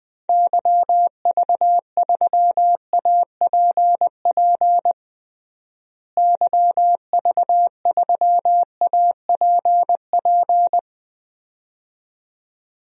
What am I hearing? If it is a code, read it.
Morse code, heard as a keyed tone, about 20 wpm, 700 Hz. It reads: YV3APP YV3APP
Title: DXCC-100 Call Signs - 6 Characters - Repeated 2x - 20wpm